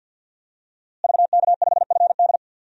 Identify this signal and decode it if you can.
Morse code, heard as a keyed tone, about 52 wpm, 700 Hz. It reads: VK5FD
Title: DXCC-100 Call Signs - 5 Characters - Repeated 3x - 50wpm